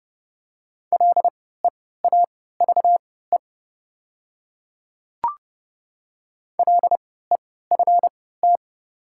Morse code, heard as a keyed tone, about 30 wpm, spaced as for 17 wpm, 700 Hz. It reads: LEAVE LEFT
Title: 3x Character Spacing - Top 500 Words - 30wpm